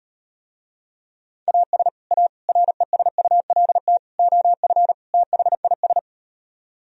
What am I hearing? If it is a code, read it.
Morse code, heard as a keyed tone, about 38 wpm, 700 Hz. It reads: AS A RESULT OF THIS